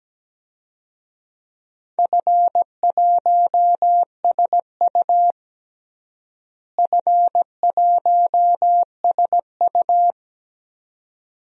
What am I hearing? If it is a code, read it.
Morse code, heard as a keyed tone, about 17 wpm, 700 Hz. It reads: F1SU F1SU
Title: DXCC-100 Call Signs - 4 Characters - Repeated 2x - 17wpm